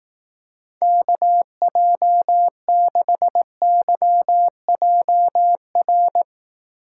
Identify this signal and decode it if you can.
Morse code, heard as a keyed tone, about 18 wpm, 700 Hz. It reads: KJ6YJR
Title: Speed Racing - USA 2x3 Call Signs 15wpm